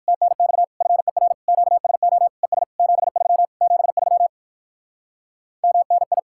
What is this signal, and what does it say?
Morse code, heard as a keyed tone, about 59 wpm, 700 Hz. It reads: TNX FER QSO ES 73 73 MNI